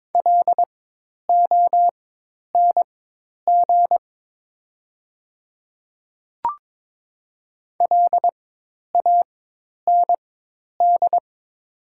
Morse code, heard as a keyed tone, about 22 wpm, spaced as for 10 wpm, 700 Hz. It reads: LONG LAND
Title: 4x Character Spacing - Top 300 Words - 22wpm